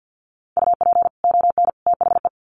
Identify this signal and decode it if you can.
Morse code, heard as a keyed tone, about 50 wpm, 700 Hz. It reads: UP ON THE
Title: Sets of 3 Words 50wpm